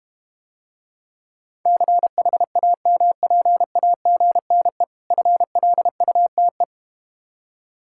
Morse code, heard as a keyed tone, about 31 wpm, 700 Hz. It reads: CHAMPAGNE FLUTE